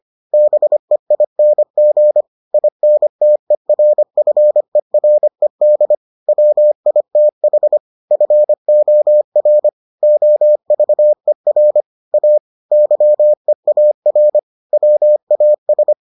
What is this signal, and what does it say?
Morse code, heard as a keyed tone, about 25 wpm, 600 Hz. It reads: BEING INTERFERED WITH FOR OVER A YEAR WAS